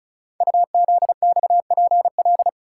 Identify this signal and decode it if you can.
Morse code, heard as a keyed tone, about 36 wpm, 700 Hz. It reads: UZXPL
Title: Code Groups - 5 Letters - 35wpm